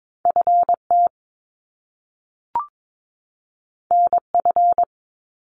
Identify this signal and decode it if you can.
Morse code, heard as a keyed tone, about 22 wpm, 700 Hz. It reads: FT NF
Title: Top Two Letter Combinations in Words 22wpm